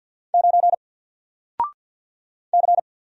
Morse code, heard as a keyed tone, about 48 wpm, 700 Hz. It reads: 9 C
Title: Single Character (Letter:Number:Punctuation:Prosign) Alt 50wpm